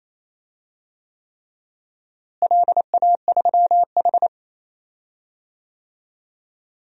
Morse code, heard as a keyed tone, about 28 wpm, 700 Hz. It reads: LA3H
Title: DXCC-100 Call Signs - 4 Characters - Repeated 3x - 28wpm